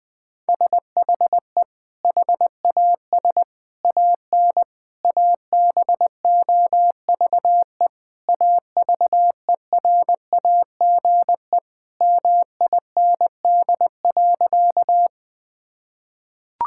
Morse code, heard as a keyed tone, about 21 wpm, 700 Hz. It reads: SHE HAS AN ABOVE AVERAGE MIND.
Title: Sentences from Top 1000 Words - Review Part 6 - 20wpm